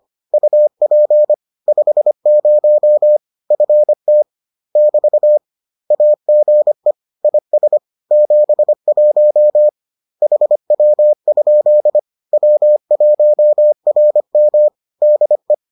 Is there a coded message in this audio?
Yes: Morse code from 0.3 to 15.5 s, heard as a keyed tone, about 25 wpm, 600 Hz. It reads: UP 50 FT <BT> AGE IS 71 HW? W1RM DE